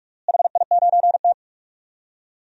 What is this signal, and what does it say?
Morse code, heard as a keyed tone, about 42 wpm, 700 Hz. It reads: HI9T
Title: DXCC-100 Call Signs - 4 Characters - Repeated 2x - 45wpm